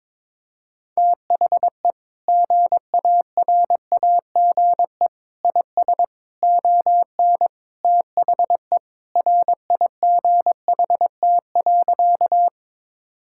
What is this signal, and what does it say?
Morse code, heard as a keyed tone, about 22 wpm, 700 Hz. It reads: THE GARAGE IS ON THE RIGHT.
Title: Sentences from Top 1600 Words 22wpm